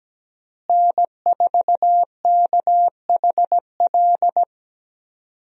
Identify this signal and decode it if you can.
Morse code, heard as a keyed tone, about 17 wpm, 700 Hz. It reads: N4KHL
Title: USA Call Signs - 5 Characters - Repeated 2x - 17wpm